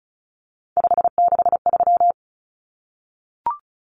Morse code, heard as a keyed tone, about 34 wpm, 700 Hz. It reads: 563
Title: Numbers - 3-Digit 35wpm